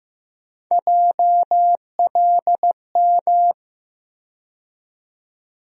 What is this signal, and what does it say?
Morse code, heard as a keyed tone, about 15 wpm, 700 Hz. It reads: JLM
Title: USA Call Sign Postfixes - 3 Letters 15wpm